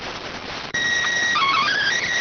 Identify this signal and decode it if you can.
SSTV (slow-scan television), PD 290.